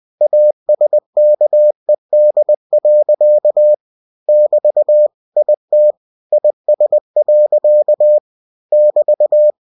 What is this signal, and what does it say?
Morse code, heard as a keyed tone, about 20 wpm, 600 Hz. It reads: ASKED. <BT> IT IS. <BT>
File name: cw-books / winnie-the-pooh / winnie-20wpm-page-0047